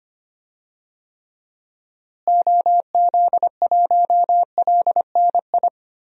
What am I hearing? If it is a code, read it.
Morse code, heard as a keyed tone, about 25 wpm, 700 Hz. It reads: OZ1LNI
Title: DXCC-100 Call Signs - 6 Characters - Repeated 3x - 25wpm